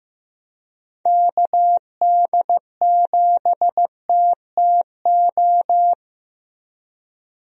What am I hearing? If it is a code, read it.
Morse code, heard as a keyed tone, about 15 wpm, 700 Hz. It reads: KD7TTO
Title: USA Call Signs - 6 Characters - Repeated 3x - 15wpm